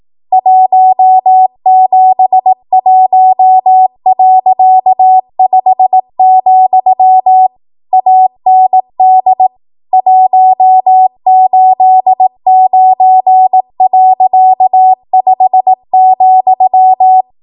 Morse code, heard as a keyed tone, about 18 wpm, 750 Hz. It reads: 171.5, AND 189.5,
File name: Archive / 18 WPM / 18WPM